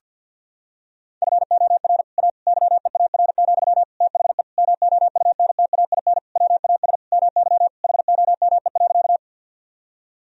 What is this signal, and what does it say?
Morse code, heard as a keyed tone, about 52 wpm, 700 Hz. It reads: FOR A YEAR, THE MOUNTAIN WAS MY HOME.
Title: Sentences from Top 1000 Words - Review Part 1 - 50wpm